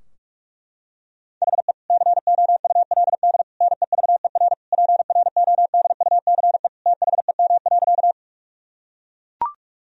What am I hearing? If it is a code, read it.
Morse code, heard as a keyed tone, about 41 wpm, 700 Hz. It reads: HE COULD NEVER PRODUCE THEM.
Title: Sentences from Top 1000 Words - Review Part 3 - 45wpm